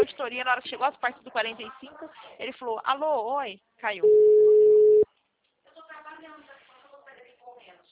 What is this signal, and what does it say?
Ringback tone (Europe, 425 Hz), with speech.